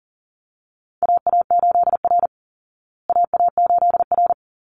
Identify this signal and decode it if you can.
Morse code, heard as a keyed tone, about 40 wpm, 700 Hz. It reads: AA8R AA8R
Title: Call Signs Repeated 3x - 40wpm